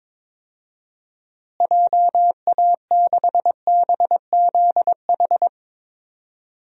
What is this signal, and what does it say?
Morse code, heard as a keyed tone, about 21 wpm, 700 Hz. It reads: JA6BZH
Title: DXCC-100 Call Signs - 6 Characters - Repeated 2x - 22wpm